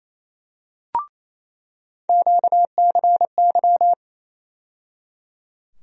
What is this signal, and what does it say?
Morse code, heard as a keyed tone, about 27 wpm, 700 Hz. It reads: QCY
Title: USA Call Sign Postfixes - 3 Letters 28wpm